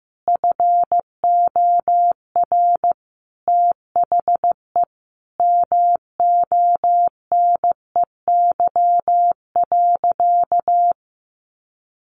Morse code, heard as a keyed tone, about 15 wpm, 700 Hz. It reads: FOR THE MONEY.
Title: Sentences from Top 600 Words 15wpm